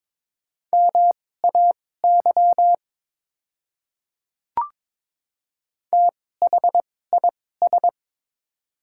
Morse code, heard as a keyed tone, about 22 wpm, spaced as for 16 wpm, 700 Hz. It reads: MAY THIS
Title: 2x Character Spacing - Top 100 Words - 22wpm